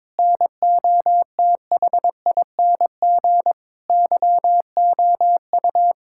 Morse code, heard as a keyed tone, about 22 wpm, 700 Hz. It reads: NOTHING YOU
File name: practice / 22wpm / Sets of 8 Words - 22wpm